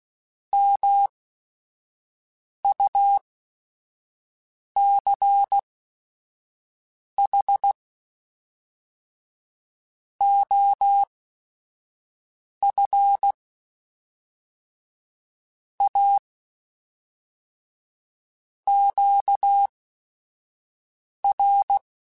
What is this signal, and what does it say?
Morse code, heard as a keyed tone, about 16 wpm, spaced as for 5 wpm, 800 Hz. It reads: MUCH OF A QR
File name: QSOs / QSOs-mp3s / WD0BGP 1@5